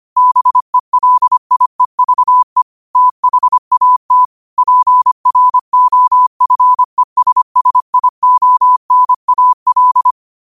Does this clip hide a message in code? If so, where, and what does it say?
Yes: Morse code from 0.2 to 10.1 s, heard as a keyed tone, about 25 wpm, 1000 Hz. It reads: BELIEVE THAT PROFESSIONAL